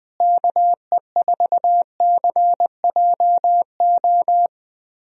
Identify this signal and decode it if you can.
Morse code, heard as a keyed tone, about 21 wpm, 700 Hz. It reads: KE4CJO
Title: USA Call Signs - 6 Characters - Repeated 2x - 20wpm